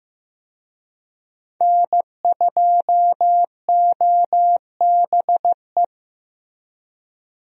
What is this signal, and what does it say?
Morse code, heard as a keyed tone, about 15 wpm, 700 Hz. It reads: N2OBE
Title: USA Call Signs - 5 Characters - Repeated 3x - 15wpm